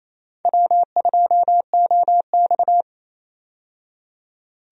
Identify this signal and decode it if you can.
Morse code, heard as a keyed tone, about 27 wpm, 700 Hz. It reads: W2OX
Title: USA Call Signs - 4 Characters - 28wpm